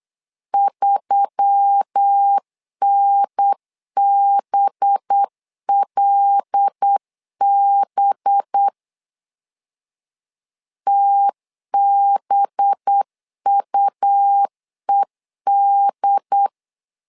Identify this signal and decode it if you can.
Morse code, heard as a keyed tone, about 8 wpm, 800 Hz. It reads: 3NBLB TBUED
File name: LEZIONI / 5 LBVC / 5 LB 8wpm 2816